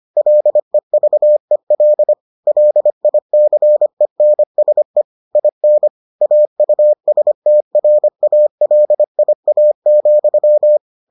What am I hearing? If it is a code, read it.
Morse code, heard as a keyed tone, about 25 wpm, 600 Hz. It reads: LEVEL LICENSE IN AUSTRALIA,